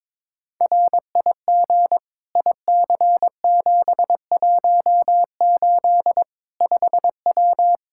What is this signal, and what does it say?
Morse code, heard as a keyed tone, about 21 wpm, 700 Hz. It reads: RIG IC718 5W
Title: K1USN SST QSO Practice 22wpm